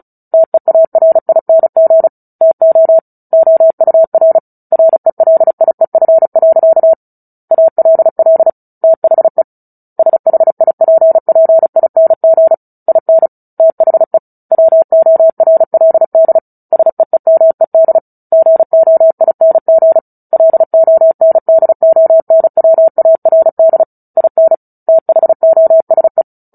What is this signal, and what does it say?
Morse code, heard as a keyed tone, about 34 wpm, 650 Hz. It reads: TEARING TO OUR RELIEF. ALL THE SHIPPING IN THE WORLD SEEMED GOING LONDONWARD IN THOSE
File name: Book Two / 09-Chapter 9 - 35 WPM